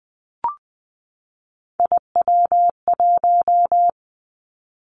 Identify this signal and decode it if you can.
Morse code, heard as a keyed tone, about 20 wpm, 700 Hz. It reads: IW1
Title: Callsign Prefixes for Top 50 DXCC - 20wpm